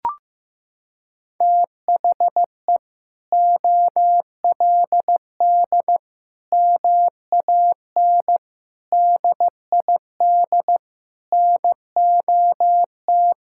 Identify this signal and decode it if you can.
Morse code, heard as a keyed tone, about 16 wpm, 700 Hz. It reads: THE OLD MAN DID NOT
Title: Sentences from Top 1600 Words 15wpm